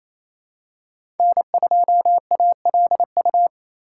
Morse code, heard as a keyed tone, about 28 wpm, 700 Hz. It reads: N2ALU
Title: USA Call Signs - 2x Character Spacing - 28wpm